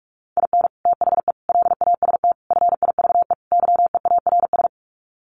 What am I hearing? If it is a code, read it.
Morse code, heard as a keyed tone, about 44 wpm, 700 Hz. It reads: IN THE LAST FIVE YEARS
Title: Sets of 5 Words Encore 45wpm